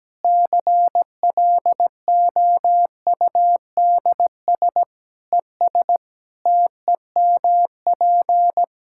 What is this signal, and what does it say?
Morse code, heard as a keyed tone, about 17 wpm, 700 Hz. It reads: CLOUDS ES TEMP
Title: Practice QSO - Exchange 2 - Info - 17wpm